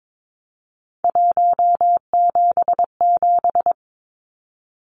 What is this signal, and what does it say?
Morse code, heard as a keyed tone, about 22 wpm, 700 Hz. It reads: 177
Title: Numbers - 3-Digit 22wpm